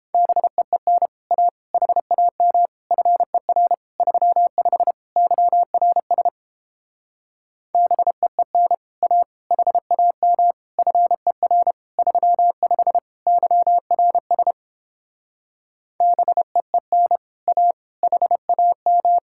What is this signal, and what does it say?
Morse code, heard as a keyed tone, about 29 wpm, 700 Hz. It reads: BEEN A HAM FER 35 YRS BEEN A HAM FER 35 YRS BEEN A HAM